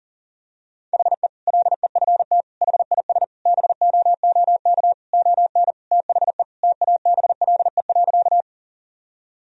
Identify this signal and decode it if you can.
Morse code, heard as a keyed tone, about 40 wpm, 700 Hz. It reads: HE LEFT HIS BOOK ON THE TABLE.
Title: Sentences from Top 1000 Words - Review Part 2 - 40wpm